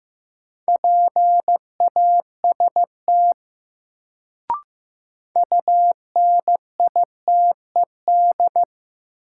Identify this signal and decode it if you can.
Morse code, heard as a keyed tone, about 15 wpm, 700 Hz. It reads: PAST UNITED
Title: Sets of 2 Words - New Word Drill - 15wpm